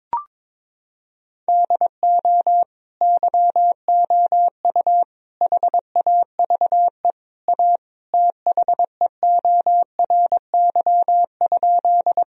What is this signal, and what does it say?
Morse code, heard as a keyed tone, about 22 wpm, 700 Hz. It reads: DO YOU HAVE A THEORY?
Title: Sentences from Top 1000 Words - Review Part 3 - 22wpm